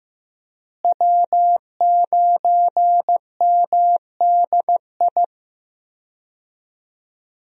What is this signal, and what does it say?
Morse code, heard as a keyed tone, about 16 wpm, 700 Hz. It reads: W9MDI
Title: USA Call Signs - 5 Characters - Repeated 3x - 15wpm